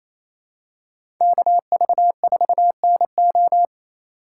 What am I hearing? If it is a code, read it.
Morse code, heard as a keyed tone, about 28 wpm, 700 Hz. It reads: KV4NO
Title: USA Call Signs - 5 Characters - 2x Character Spacing - 28wpm